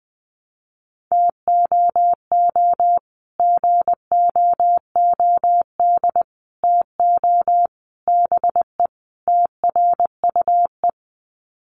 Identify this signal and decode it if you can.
Morse code, heard as a keyed tone, about 20 wpm, 700 Hz. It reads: TOO GOOD TO BE TRUE
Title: Sets of 5 Words Encore 20wpm